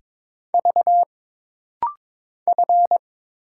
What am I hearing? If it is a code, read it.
Morse code, heard as a keyed tone, about 22 wpm, 700 Hz. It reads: V F
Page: This rapid-fire timing is geared to push your ICR (Instant Character Recognition) skills.